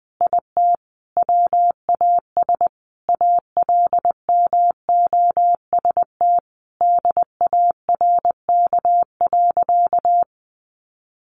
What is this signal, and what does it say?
Morse code, heard as a keyed tone, about 20 wpm, 700 Hz. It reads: IT WAS ALMOST DARK.
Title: Sentences from Top 700 Words 20wpm